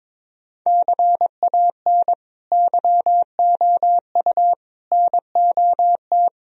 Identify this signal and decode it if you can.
Morse code, heard as a keyed tone, about 22 wpm, 700 Hz. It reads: CAN YOU NOT
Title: Sentences from Top 1000 Words - Review Part 2 - 22wpm